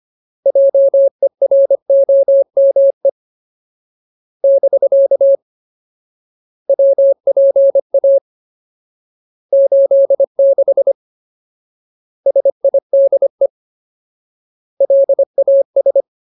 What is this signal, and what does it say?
Morse code, heard as a keyed tone, about 25 wpm, 550 Hz. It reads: JEROME <BK> WPA 86 SIDE LAS
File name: Ultimate Word List / 25WPM Ultimate